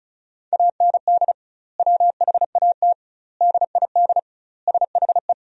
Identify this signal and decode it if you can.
Morse code, heard as a keyed tone, about 34 wpm, 700 Hz. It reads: AND WHAT DID SHE
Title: 2x Word-Spacing - Sentences from Top 200 Words - 35wpm